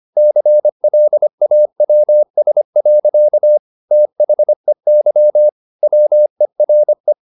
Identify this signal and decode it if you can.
Morse code, heard as a keyed tone, about 25 wpm, 600 Hz. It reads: CLAWS. THEY WERE